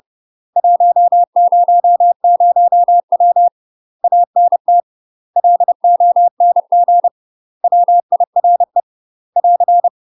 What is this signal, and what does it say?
Morse code, heard as a keyed tone, about 30 wpm, 700 Hz. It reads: 100W ANT LONG WIRE <AR>
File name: UK MASTER QSOs TRAINING / 30WPM / UK MASTER QSOs 0042